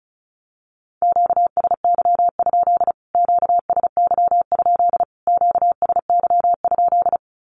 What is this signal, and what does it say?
Morse code, heard as a keyed tone, about 34 wpm, 700 Hz. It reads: QSY? QSY? QSY?
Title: Q-Codes Repeated 3x 35wpm